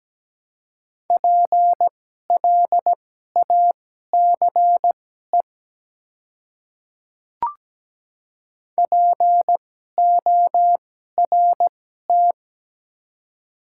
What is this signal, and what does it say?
Morse code, heard as a keyed tone, about 17 wpm, spaced as for 12 wpm, 700 Hz. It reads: PLACE PORT
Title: 2x Character Spacing - Top 300 Words - 17wpm